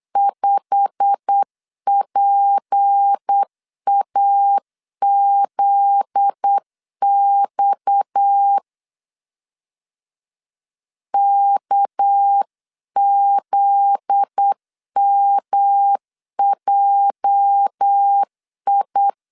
Morse code, heard as a keyed tone, about 8 wpm, 800 Hz. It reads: 5PAZX KZMJI
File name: CW / DET / 1 9 1500C S B 8wpm 5624